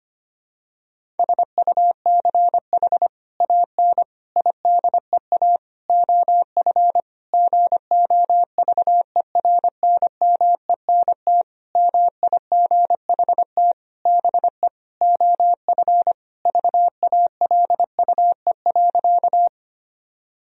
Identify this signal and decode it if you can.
Morse code, heard as a keyed tone, about 25 wpm, 700 Hz. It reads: SUCH AN IDEA OF GOVERNMENT MIGHT BE OF VALUE.
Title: Sentences from Top 1000 Words - Review Part 3 - 25wpm